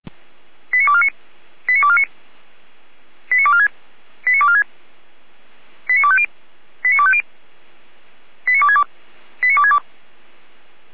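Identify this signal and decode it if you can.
Five-tone selective call (ZVEI), heard as quick runs of five tones, tones of about 70 ms. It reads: 89249 89249 89246 89246 89250 89250 89262 89262
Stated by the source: ZVEI five-tone selective call received by SDR.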